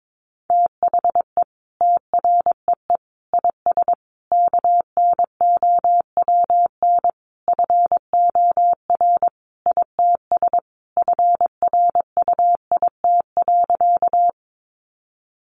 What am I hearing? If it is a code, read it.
Morse code, heard as a keyed tone, about 22 wpm, 700 Hz. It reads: THE TREE IS KNOWN FOR ITS FRUIT.
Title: Sentences from Top 1300 Words 22wpm